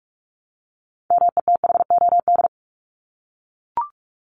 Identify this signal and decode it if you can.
Morse code, heard as a keyed tone, about 44 wpm, 700 Hz. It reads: METHOD
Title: Top 501-750 Words 45wpm